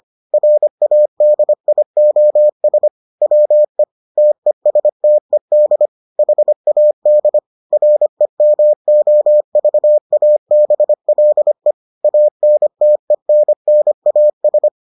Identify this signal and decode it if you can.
Morse code, heard as a keyed tone, about 25 wpm, 600 Hz. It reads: RADIOS WE TESTED HAD REMOVABLE ANTENNAS